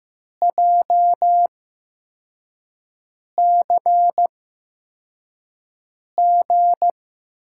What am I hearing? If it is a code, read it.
Morse code, heard as a keyed tone, about 15 wpm, spaced as for 4 wpm, 700 Hz. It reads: JCG
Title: USA Call Signs - 6 Characters - 8x Character Spacing - 15wpm